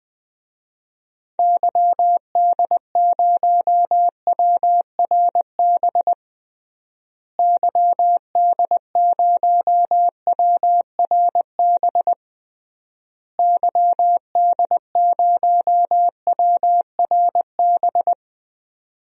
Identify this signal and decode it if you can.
Morse code, heard as a keyed tone, about 20 wpm, 700 Hz. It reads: YD0WRB YD0WRB YD0WRB
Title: DXCC-100 Call Signs - 6 Characters - Repeated 3x - 20wpm